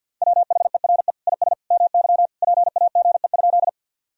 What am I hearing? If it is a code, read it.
Morse code, heard as a keyed tone, about 51 wpm, 700 Hz. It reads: WHERE IS MY PAGE?
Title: Sentences from Top 1000 Words - Review Part 1 - 50wpm